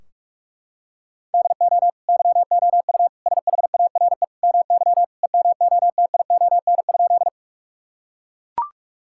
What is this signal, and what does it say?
Morse code, heard as a keyed tone, about 43 wpm, 700 Hz. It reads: DO YOU SHARE MY EMOTION?
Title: Sentences from Top 1600 Words 45wpm